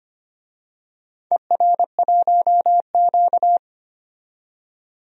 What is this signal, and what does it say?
Morse code, heard as a keyed tone, about 25 wpm, 700 Hz. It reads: ER1Q
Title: DXCC-100 Call Signs - 4 Characters - 25wpm